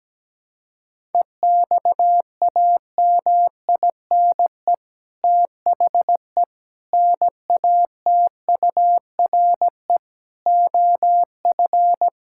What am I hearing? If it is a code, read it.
Morse code, heard as a keyed tone, about 17 wpm, 700 Hz. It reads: EXAMINE THE NATURE OF
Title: Sentences from Top 1600 Words 17wpm